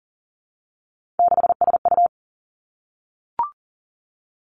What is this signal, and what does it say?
Morse code, heard as a keyed tone, about 42 wpm, 700 Hz. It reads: 6SU
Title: Three Challenging Letter-Number Combinations 40wpm